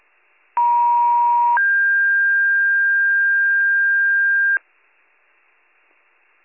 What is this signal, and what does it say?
Two-tone page, heard as a one-second tone, then a three-second tone, about 950 then 1600 Hz. It is heard from 0.6 to 4.6 s.